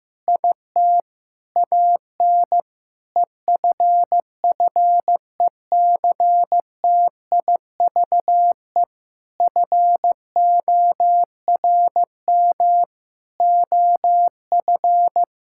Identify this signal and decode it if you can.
Morse code, heard as a keyed tone, about 15 wpm, 700 Hz. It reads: IT AN EFFECTIVE FORM OF
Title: Sentences from Top 1000 Words - Review Part 6 - 15wpm